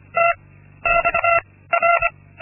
Morse code, heard as a keyed tone, about 24 wpm, 680 Hz. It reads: T X R
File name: Linksmakalnis-2